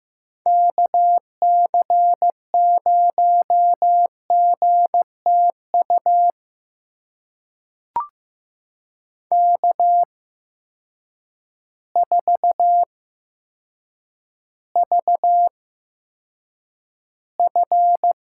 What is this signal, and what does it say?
Morse code, heard as a keyed tone, about 15 wpm, spaced as for 8 wpm, 700 Hz. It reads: KC0GTU K4VF
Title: USA Call Signs - 8x Character Spacing - 15wpm